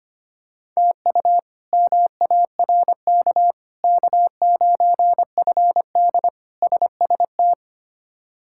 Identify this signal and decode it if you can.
Morse code, heard as a keyed tone, about 25 wpm, 700 Hz. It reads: TU MARK K9FD SST
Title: K1USN SST QSO Practice - 25wpm